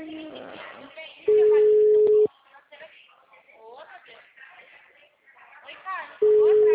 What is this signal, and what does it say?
Ringback tone (Europe, 425 Hz), with speech.
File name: out- -2011- - - .321